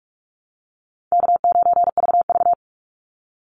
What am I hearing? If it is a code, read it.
Morse code, heard as a keyed tone, about 44 wpm, 700 Hz. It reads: K9VV
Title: Call Signs Repeated 2x - 45wpm